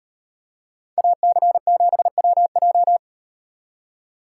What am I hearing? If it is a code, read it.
Morse code, heard as a keyed tone, about 37 wpm, 700 Hz. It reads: AC7WJ